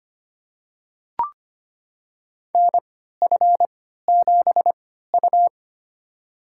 Morse code, heard as a keyed tone, about 25 wpm, spaced as for 14 wpm, 700 Hz. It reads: NF7U
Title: USA Call Signs - 4 Characters - 3x Character Spacing - 25wpm